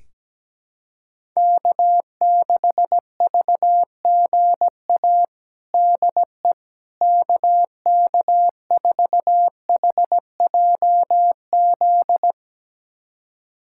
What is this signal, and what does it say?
Morse code, heard as a keyed tone, about 17 wpm, 700 Hz. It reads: K6VGA DE KK4HJZ
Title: USA Callsign DE Callsign 17wpm